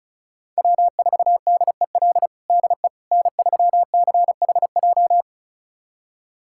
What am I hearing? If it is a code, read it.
Morse code, heard as a keyed tone, about 36 wpm, 690 Hz. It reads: W4DEL DE N3CHJ
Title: USA Callsign DE Callsign - 5 Characters - 35wpm